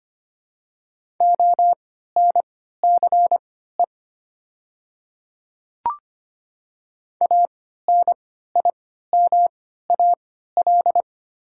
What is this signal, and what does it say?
Morse code, heard as a keyed tone, about 25 wpm, spaced as for 14 wpm, 700 Hz. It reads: ONCE ANIMAL
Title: 3x Character Spacing - Top 300 Words - 25wpm